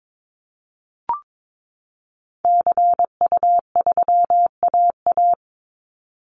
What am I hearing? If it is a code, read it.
Morse code, heard as a keyed tone, about 22 wpm, 700 Hz. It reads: CU3AA
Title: Call Signs Repeated 3x - 22wpm